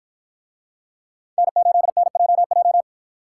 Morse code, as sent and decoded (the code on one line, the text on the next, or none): -. ---.. -. .--- .---
N8NJJ